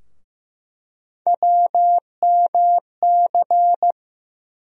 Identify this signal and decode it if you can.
Morse code, heard as a keyed tone, about 15 wpm, 700 Hz. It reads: WMC
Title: USA Call Sign Postfixes - 3 Letters 15wpm